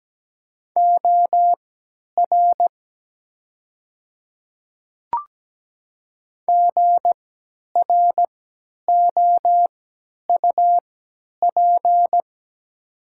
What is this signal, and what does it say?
Morse code, heard as a keyed tone, about 17 wpm, spaced as for 10 wpm, 700 Hz. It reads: OR GROUP
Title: 3x Character Spacing - Top 500 Words - 17wpm